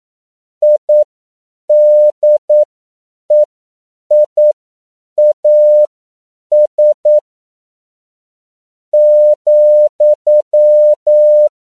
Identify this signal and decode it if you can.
Morse code, heard as a keyed tone, about 9 wpm, spaced as for 7 wpm, 600 Hz. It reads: IDEIAS ,
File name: CW / 10PPM